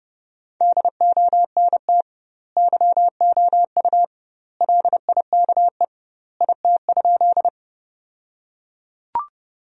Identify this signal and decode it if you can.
Morse code, heard as a keyed tone, about 32 wpm, 700 Hz. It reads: DONT YOU LIKE IT?
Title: 2x Word-Spacing - Sentences from Top 300 Words - 30wpm